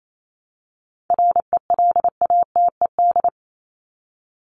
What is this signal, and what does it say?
Morse code, heard as a keyed tone, about 28 wpm, 700 Hz. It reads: RELATED
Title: ED Suffix - Top 100 Words with Base Words 28wpm